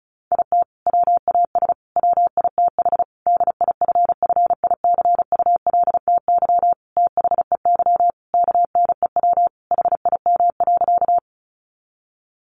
Morse code, heard as a keyed tone, about 34 wpm, 700 Hz. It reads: IT WAS WITH DIFFICULTY THEY KNEW HIM.
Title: Sentences from Top 1500 Words 35wpm